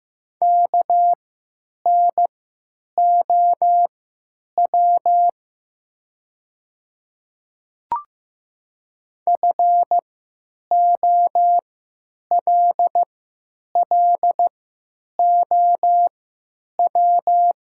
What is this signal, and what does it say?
Morse code, heard as a keyed tone, about 15 wpm, spaced as for 9 wpm, 700 Hz. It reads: KNOW FOLLOW
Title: 3x Character Spacing - Top 300 Words - 15wpm